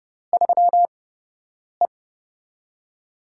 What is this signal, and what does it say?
Morse code, heard as a keyed tone, about 30 wpm, 700 Hz. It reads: 3 E
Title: USA Call Signs - 4 Characters - 8x Character Spacing - 30wpm